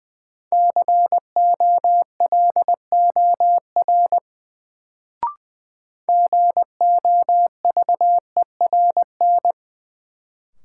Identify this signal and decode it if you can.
Morse code, heard as a keyed tone, about 20 wpm, 700 Hz. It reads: COLOR GOVERN
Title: Sentences from Top 1000 Words - Review Part 2 - New Word Drill - 20wpm